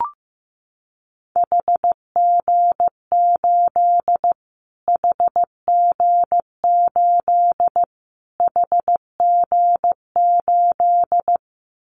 Morse code, heard as a keyed tone, about 15 wpm, 700 Hz. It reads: HG8 HG8 HG8
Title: Callsign Prefixes for Top 25 DXCC - Repeated 3x - 15wpm